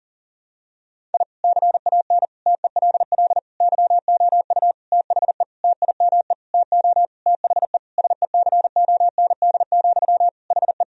Morse code, heard as a keyed tone, about 40 wpm, 700 Hz. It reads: I CAN TELL YOU THE TIME TO THE SECOND, HE
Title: Sentences from Top 1000 Words - Review Part 1 - 40wpm